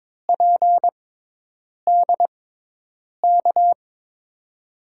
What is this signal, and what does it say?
Morse code, heard as a keyed tone, about 23 wpm, spaced as for 8 wpm, 700 Hz. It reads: PDK
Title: USA Call Signs - 5 Characters - 6x Character Spacing - 22wpm